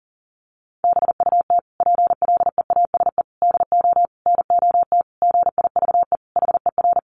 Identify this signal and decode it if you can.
Morse code, heard as a keyed tone, about 40 wpm, 700 Hz. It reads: BUT PLEASE DO NOT GIVE HER
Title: Sentences from Top 1500 Words 40wpm